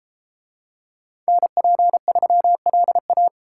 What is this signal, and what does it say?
Morse code, heard as a keyed tone, about 33 wpm, 700 Hz. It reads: NP3LA